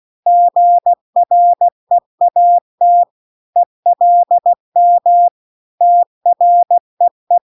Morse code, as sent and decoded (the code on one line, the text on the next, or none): --. .-. .- - / . .-.. -- / - .-. .
GREAT ELM TREE